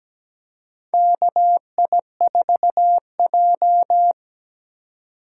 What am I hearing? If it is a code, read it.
Morse code, heard as a keyed tone, about 17 wpm, 700 Hz. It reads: KI4J
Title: USA Call Signs - 4 Characters - 4x Character Spacing - 17wpm